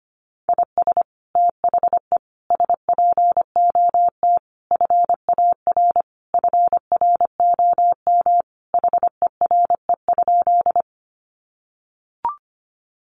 Morse code, heard as a keyed tone, about 25 wpm, 700 Hz. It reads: IS THE SPOT FAR FROM HERE?
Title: Sentences from Top 1100 Words 25wpm